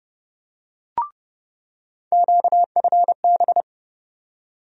Morse code, heard as a keyed tone, about 30 wpm, 700 Hz. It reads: QFB
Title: USA Call Sign Postfixes - 3 Letters 30wpm